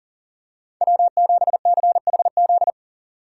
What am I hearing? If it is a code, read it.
Morse code, heard as a keyed tone, about 40 wpm, 700 Hz. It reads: W7CHZ
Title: USA Call Signs - 5 Characters - 2x Character Spacing - 40wpm